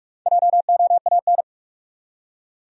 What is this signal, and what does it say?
Morse code, heard as a keyed tone, about 41 wpm, 700 Hz. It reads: JOAN
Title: USA Common Names - 45wpm